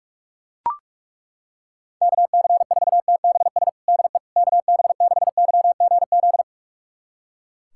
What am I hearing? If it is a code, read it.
Morse code, heard as a keyed tone, about 41 wpm, 700 Hz. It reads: KC4TBS DE KB6YGZ
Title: USA Callsign DE Callsign - 6 Characters - 45wpm